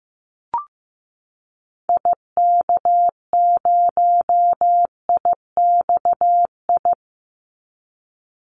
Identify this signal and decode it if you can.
Morse code, heard as a keyed tone, about 15 wpm, 700 Hz. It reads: IK0IXI
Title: Call Signs Repeated 3x - 15wpm